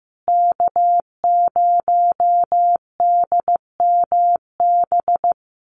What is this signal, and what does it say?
Morse code, heard as a keyed tone, about 15 wpm, 700 Hz. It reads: K0DMB
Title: USA Call Signs Repeated 2x 15wpm